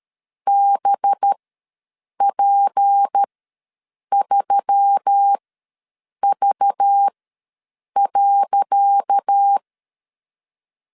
Morse code, heard as a keyed tone, about 13 wpm, spaced as for 7 wpm, 800 Hz. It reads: BP3V.
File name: LEZIONI / 10 SIGNS / 10C DOT ws 12wpm 3548